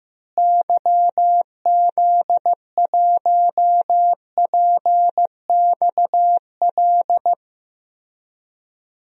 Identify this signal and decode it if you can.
Morse code, heard as a keyed tone, about 15 wpm, 700 Hz. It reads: YZ1PXL
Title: DXCC-100 Call Signs - 6 Characters - Repeated 3x - 15wpm